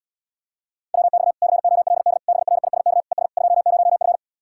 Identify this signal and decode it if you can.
Morse code, heard as a keyed tone, about 77 wpm, 700 Hz. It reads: WX COLD BRISK ES 20F